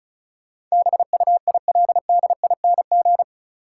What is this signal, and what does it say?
Morse code, heard as a keyed tone, about 36 wpm, 700 Hz. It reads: BUILDING
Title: Sentences from Top 700 Words - New Word Drill - 35wpm